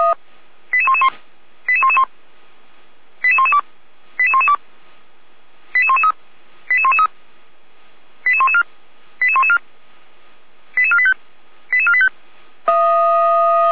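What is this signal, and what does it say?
Five-tone selective call (ZVEI), heard as quick runs of five tones, tones of about 70 ms. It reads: 88111 88111 88112 88112 88113 88113 88114 88114 88485 88485
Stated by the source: Five-tone ZVEI selective call received by SDR.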